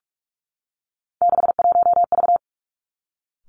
Morse code, heard as a keyed tone, about 46 wpm, 700 Hz. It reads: 61V
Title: Three Challenging Letter-Number Combinations 45wpm